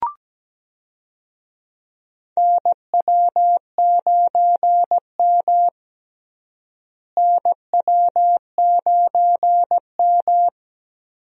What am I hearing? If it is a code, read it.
Morse code, heard as a keyed tone, about 17 wpm, 700 Hz. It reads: NW9M NW9M
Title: USA Call Signs - 4 Characters - Repeated 2x - 17wpm